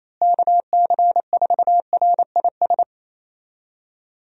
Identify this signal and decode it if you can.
Morse code, heard as a keyed tone, about 27 wpm, 700 Hz. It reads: KC4RIS
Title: USA Call Signs - 6 Characters - Repeated 3x - 28wpm